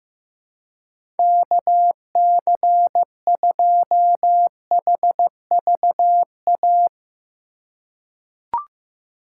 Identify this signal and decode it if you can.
Morse code, heard as a keyed tone, about 15 wpm, 700 Hz. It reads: KC2HVA
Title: USA Call Signs - 6 Characters - 6x Character Spacing - 15wpm